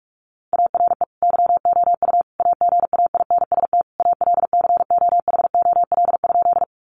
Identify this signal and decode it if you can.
Morse code, heard as a keyed tone, about 44 wpm, 700 Hz. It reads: ARE YOU AGAINST ALCOHOL?
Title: Sentences from Top 1400 Words 45wpm